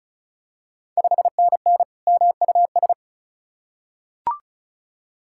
Morse code, heard as a keyed tone, about 36 wpm, 700 Hz. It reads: 5NN MUS
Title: Ontario QSO Party Exchange 35wpm